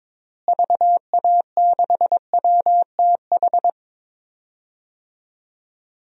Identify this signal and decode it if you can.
Morse code, heard as a keyed tone, about 22 wpm, 700 Hz. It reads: VA6WTH
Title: DXCC-100 Call Signs - 6 Characters - Repeated 2x - 22wpm